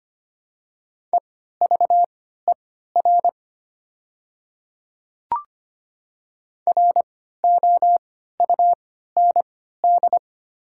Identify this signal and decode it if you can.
Morse code, heard as a keyed tone, about 25 wpm, spaced as for 14 wpm, 700 Hz. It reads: EVER ROUND
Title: 3x Character Spacing - Top 500 Words - 25wpm